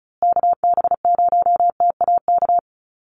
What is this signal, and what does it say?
Morse code, heard as a keyed tone, about 34 wpm, 700 Hz. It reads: KB0TAK
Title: USA Call Signs Repeated 3x 35wpm